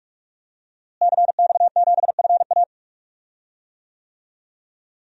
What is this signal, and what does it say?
Morse code, heard as a keyed tone, about 46 wpm, 700 Hz. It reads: CX7FA
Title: DXCC-100 Call Signs - 5 Characters - Repeated 3x - 45wpm